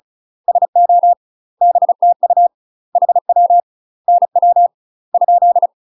Morse code, heard as a keyed tone, about 36 wpm, 700 Hz. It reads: SO BTU HW NW ?
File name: UK MASTER QSOs TRAINING / 35WPM / UK MASTER QSOs 0018